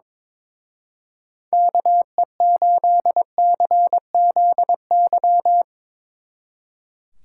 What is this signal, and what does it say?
Morse code, heard as a keyed tone, about 24 wpm, 700 Hz. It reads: KE8CZY
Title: Speed Racing - USA 2x3 Call Signs 22wpm